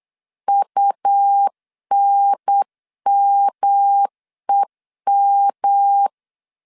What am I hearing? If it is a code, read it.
Morse code, heard as a keyed tone, about 8 wpm, 800 Hz. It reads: UNMEM
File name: CW / DET / 1 3 1500C S B 8wpm 4141